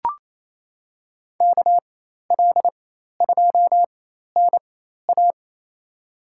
Morse code, heard as a keyed tone, about 28 wpm, spaced as for 13 wpm, 700 Hz. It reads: KL2NA
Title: USA Call Signs - 5 Characters - 4x Character Spacing - 28wpm